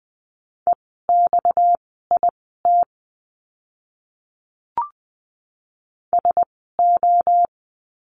Morse code, heard as a keyed tone, about 20 wpm, spaced as for 15 wpm, 700 Hz. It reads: EXIT SO
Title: 230 Lesson 37f - Letter x - Words 20wpm